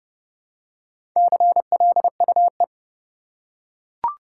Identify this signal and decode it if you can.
Morse code, heard as a keyed tone, about 30 wpm, 700 Hz. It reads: CLUE
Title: Four Letter Words - 30wpm